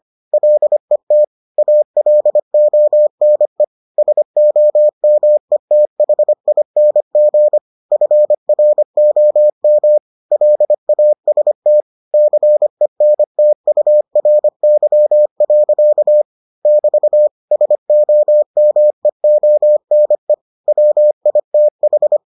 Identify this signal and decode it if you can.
Morse code, heard as a keyed tone, about 25 wpm, 600 Hz. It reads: LET ALONE SOMETHING FROM LAST CENTURY. <BT> SOMEONE WITH